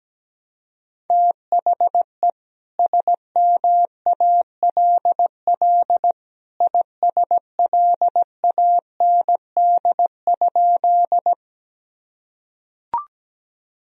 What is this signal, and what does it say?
Morse code, heard as a keyed tone, about 17 wpm, 700 Hz. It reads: THE SMALL ISLAND?
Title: Sentences from Top 1000 Words - Review Part 3 - 17wpm